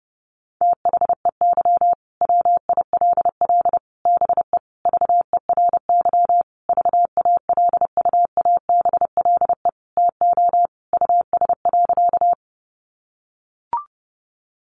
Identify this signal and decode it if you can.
Morse code, heard as a keyed tone, about 28 wpm, 700 Hz. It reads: THEY WILL BE VERY VALUABLE TO US.
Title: Sentences from Top 1100 Words 30wpm